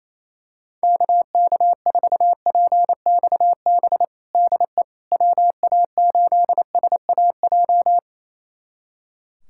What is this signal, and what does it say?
Morse code, heard as a keyed tone, about 27 wpm, 700 Hz. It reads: KK4PXB DE WA8SAJ
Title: USA Callsign DE Callsign - 6 Characters - 28wpm